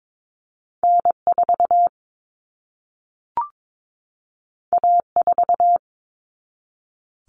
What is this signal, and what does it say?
Morse code, heard as a keyed tone, about 22 wpm, 700 Hz. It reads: N4 A4
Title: Common Cut Numbers - 2-Digit 22wpm